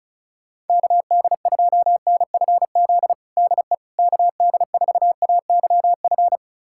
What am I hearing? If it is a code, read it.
Morse code, heard as a keyed tone, about 36 wpm, 700 Hz. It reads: KD2NFZ DE KD4AYF
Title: USA Callsign DE Callsign - 6 Characters - 35wpm